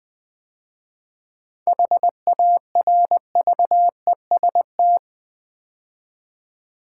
Morse code, heard as a keyed tone, about 20 wpm, 700 Hz. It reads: HARVEST
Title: 7 Letter Words - First 3 Letters Given - 20wpm